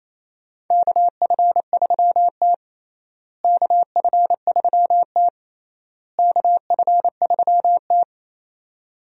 Morse code, heard as a keyed tone, about 28 wpm, 700 Hz. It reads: KF3T KF3T KF3T
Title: USA Call Signs - 4 Characters - Repeated 3x - 28wpm